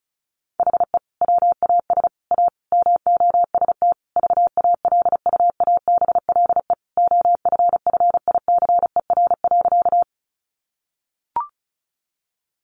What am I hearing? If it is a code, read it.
Morse code, heard as a keyed tone, about 34 wpm, 700 Hz. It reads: HE WAS A MOST VALUABLE OFFICER.
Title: Sentences from Top 1200 Words 35wpm